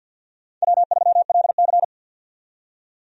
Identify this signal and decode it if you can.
Morse code, heard as a keyed tone, about 50 wpm, 700 Hz. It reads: W3LC
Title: USA Call Signs - 4 Characters - 50wpm